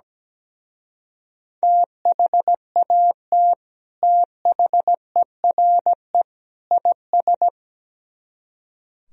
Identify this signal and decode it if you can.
Morse code, heard as a keyed tone, about 17 wpm, 700 Hz. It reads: THAT THERE IS
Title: Speed-Racing - Sets of 3 Words 17wpm